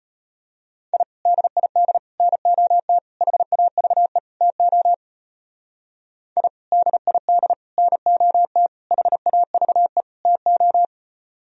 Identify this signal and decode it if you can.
Morse code, heard as a keyed tone, about 34 wpm, 700 Hz. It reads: I DID NOT HAVE TO I DID NOT HAVE TO